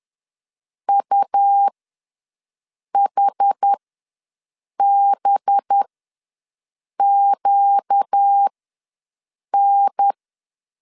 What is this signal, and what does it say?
Morse code, heard as a keyed tone, about 10 wpm, spaced as for 6 wpm, 800 Hz. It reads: UHBQN
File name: LEZIONI / 8 YJ45 / 8 YJ ws 10wpm 4043